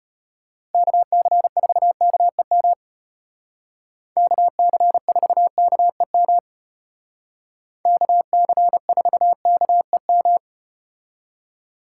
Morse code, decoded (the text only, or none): KC4KEM KC4KEM KC4KEM